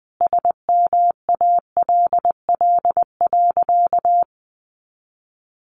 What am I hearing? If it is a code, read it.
Morse code, heard as a keyed tone, about 20 wpm, 700 Hz. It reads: SMALL.
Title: Additional Easy Phrases 20wpm